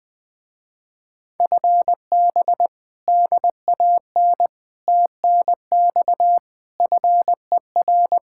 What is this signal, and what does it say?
Morse code, heard as a keyed tone, about 20 wpm, 700 Hz. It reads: FB DAN TNX FER
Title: Speed-Racing Common QSO Phrases - 20wpm